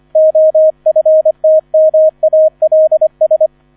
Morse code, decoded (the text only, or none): OFTMALS